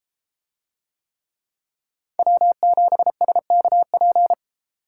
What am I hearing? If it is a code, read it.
Morse code, heard as a keyed tone, about 32 wpm, 700 Hz. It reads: W7SKP